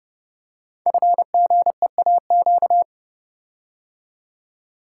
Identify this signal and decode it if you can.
Morse code, heard as a keyed tone, about 30 wpm, 700 Hz. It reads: FGEAQ
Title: Code Groups - 5 Letters - 30wpm